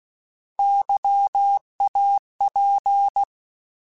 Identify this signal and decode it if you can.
Morse code, heard as a keyed tone, about 16 wpm, 800 Hz. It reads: YAP
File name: Alphabetical / Alphabetical mp3s / Ys @15